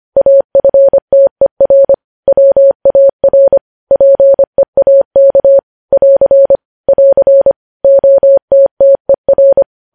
Morse code, heard as a keyed tone, about 25 wpm, 550 Hz. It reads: AFTER WAR PEAK <AR> <AR> OTTER
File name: podcast / News headlines-1x-25x25